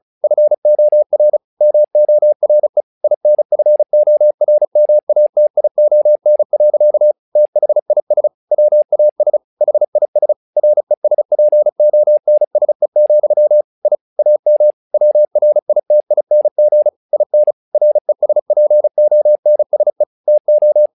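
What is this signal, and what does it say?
Morse code, heard as a keyed tone, about 34 wpm, 600 Hz. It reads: FOR MORE INFORMATION. THIS WAS HIS RESPONSE, I AM WRITING IN RESPONSE TO
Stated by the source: Morse practice text "Tower Trouble" sent at 35 WPM, from the SKCC - Straight Key Century Club learning center audio files.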